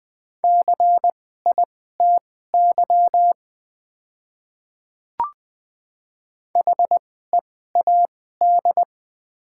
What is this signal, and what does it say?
Morse code, heard as a keyed tone, about 20 wpm, spaced as for 15 wpm, 700 Hz. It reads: CITY HEAD
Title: 2x Character Spacing - Top 500 Words - 20wpm